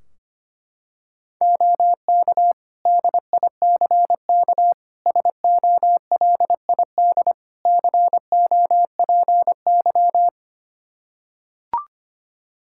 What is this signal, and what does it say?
Morse code, heard as a keyed tone, about 25 wpm, 700 Hz. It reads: OK DICK SOLID COPY
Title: Practice QSO - Exchange 3 - More Info - 25wpm